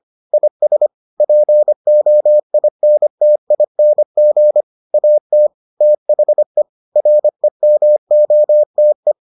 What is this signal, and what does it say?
Morse code, heard as a keyed tone, about 25 wpm, 600 Hz. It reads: IS POINTING AT THE REMOTE